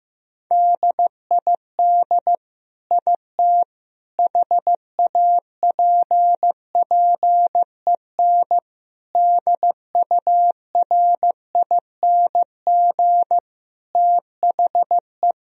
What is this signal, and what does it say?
Morse code, heard as a keyed tone, about 15 wpm, 700 Hz. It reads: DID IT HAPPEN DURING THE
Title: Sentences from Top 1000 Words - Review Part 2 - 15wpm